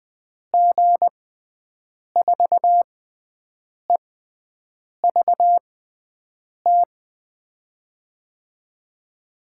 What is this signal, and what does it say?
Morse code, heard as a keyed tone, about 20 wpm, spaced as for 7 wpm, 700 Hz. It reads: G4EVT
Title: USA Call Signs - 6x Character Spacing - 20wpm